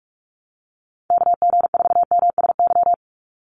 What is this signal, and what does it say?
Morse code, heard as a keyed tone, about 44 wpm, 700 Hz. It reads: KG4MSY
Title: USA Call Signs 45wpm